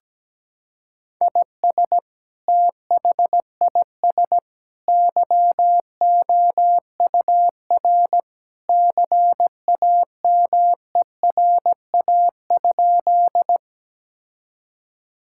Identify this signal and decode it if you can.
Morse code, heard as a keyed tone, about 17 wpm, 700 Hz. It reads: IS THIS YOUR CAMERA?
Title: Sentences from Top 1000 Words - Review Part 5 - 17wpm